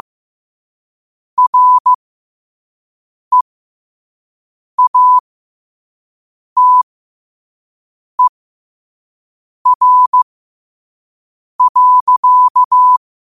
Morse code, heard as a keyed tone, about 15 wpm, spaced as for 5 wpm, 1000 Hz. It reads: REATER.
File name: qotdfiles / 5wpm